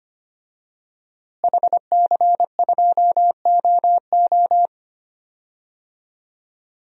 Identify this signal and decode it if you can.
Morse code, heard as a keyed tone, about 25 wpm, 700 Hz. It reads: HC2OO
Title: DXCC-100 Call Signs - 5 Characters - Repeated 2x - 25wpm